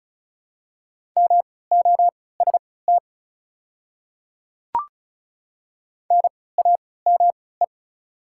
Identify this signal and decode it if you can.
Morse code, heard as a keyed tone, about 36 wpm, spaced as for 20 wpm, 700 Hz. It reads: MOST NAME
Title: 3x Character Spacing - Top 200 Words - 35wpm